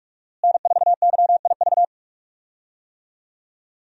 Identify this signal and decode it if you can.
Morse code, heard as a keyed tone, about 46 wpm, 700 Hz. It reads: N4YIV
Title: USA Call Signs - 5 Characters - Repeated 2x - 45wpm